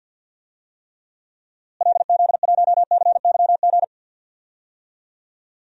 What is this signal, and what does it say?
Morse code, heard as a keyed tone, about 48 wpm, 700 Hz. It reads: LZ1CYG
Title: DXCC-100 Call Signs - 6 Characters - Repeated 2x - 50wpm